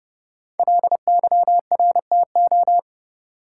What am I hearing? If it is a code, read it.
Morse code, heard as a keyed tone, about 30 wpm, 700 Hz. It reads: LYRTO
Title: Code Groups - 5 Letters - 30wpm